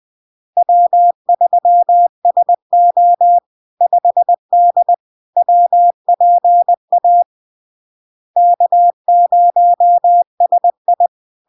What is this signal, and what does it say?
Morse code, heard as a keyed tone, about 20 wpm, 700 Hz. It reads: W3SO 5D WPA K0SI
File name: LARK-FD-2022-Log-in-CW-20wpm-700-Hz